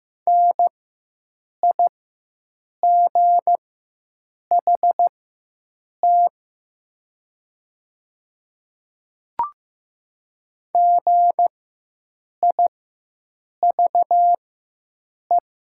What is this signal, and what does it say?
Morse code, heard as a keyed tone, about 15 wpm, spaced as for 7 wpm, 700 Hz. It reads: NIGHT GIVE
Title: 4x Character Spacing - Top 300 Words - 15wpm